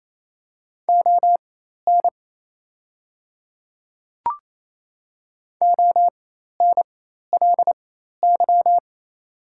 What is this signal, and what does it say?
Morse code, heard as a keyed tone, about 27 wpm, spaced as for 13 wpm, 700 Hz. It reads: ON ONLY
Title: 4x Character Spacing - Top 300 Words - 28wpm